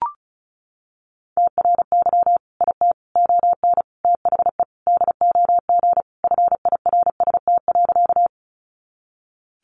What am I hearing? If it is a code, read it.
Morse code, heard as a keyed tone, about 34 wpm, 700 Hz. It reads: TRY IT ON THE DOG FIRST.
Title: Sentences from Top 400 Words 35wpm